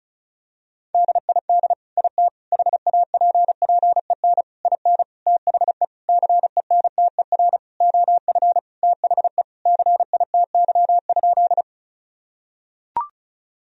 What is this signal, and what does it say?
Morse code, heard as a keyed tone, about 34 wpm, 700 Hz. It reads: DID IT HAPPEN IN THE CENTER OF THE CITY?
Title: Sentences from Top 1000 Words - Review Part 2 - 35wpm